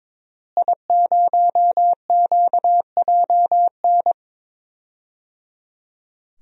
Morse code, heard as a keyed tone, about 22 wpm, 700 Hz. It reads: I0QJN
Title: DXCC-100 Call Signs - 5 Characters - Repeated 3x - 22wpm